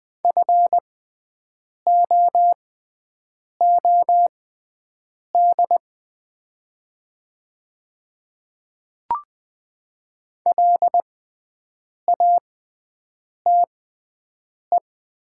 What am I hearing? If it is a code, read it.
Morse code, heard as a keyed tone, about 20 wpm, spaced as for 7 wpm, 700 Hz. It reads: FOOD LATE
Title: 6x Character Spacing - Top 300 Words - 20wpm